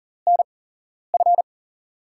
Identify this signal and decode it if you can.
Morse code, heard as a keyed tone, about 39 wpm, spaced as for 11 wpm, 700 Hz. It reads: NF
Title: USA Call Signs - 5 Characters - 8x Character Spacing - 40wpm